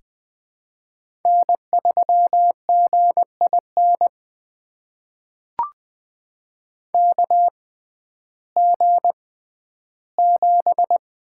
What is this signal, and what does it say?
Morse code, heard as a keyed tone, about 20 wpm, spaced as for 12 wpm, 700 Hz. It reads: N3GIN KG7
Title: USA Call Signs - 6x Character Spacing - 20wpm